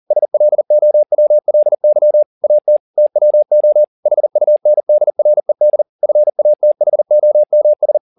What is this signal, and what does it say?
Morse code, heard as a keyed tone, about 40 wpm, 600 Hz. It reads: SLOWLY AT TWO HUNDRED FATHOMS
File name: the old man and the sea 2 40wpm